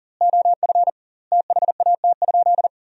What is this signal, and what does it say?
Morse code, heard as a keyed tone, about 40 wpm, 700 Hz. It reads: OF THAT?
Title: 2x Word-Spacing - Sentences from Top 100 Words - 40wpm